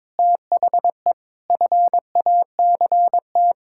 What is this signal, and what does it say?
Morse code, heard as a keyed tone, about 22 wpm, 700 Hz. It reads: THE FACT
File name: Sentences with 7-Word Phrases - 22wpm